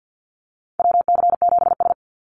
Morse code, heard as a keyed tone, about 50 wpm, 700 Hz. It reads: WC7S
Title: Call Signs - 50wpm